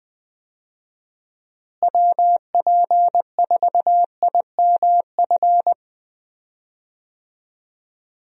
Morse code, heard as a keyed tone, about 20 wpm, 700 Hz. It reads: WP4IMF
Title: USA Call Signs - 6 Characters - Repeated 2x - 20wpm